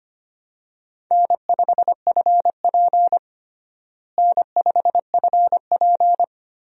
Morse code, heard as a keyed tone, about 25 wpm, 700 Hz. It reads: N5FP N5FP
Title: USA Call Signs - 4 Characters - Repeated 2x - 25wpm